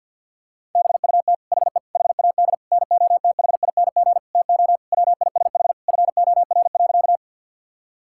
Morse code, heard as a keyed tone, about 50 wpm, 700 Hz. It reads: BUT HE HAD NOTHING TO WISH FOR.
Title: Sentences from Top 1600 Words 50wpm